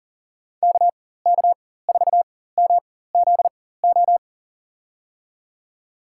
Morse code, heard as a keyed tone, about 40 wpm, spaced as for 19 wpm, 700 Hz. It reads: KK4MZO
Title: USA Call Signs - 6 Characters - 4x Character Spacing - 40wpm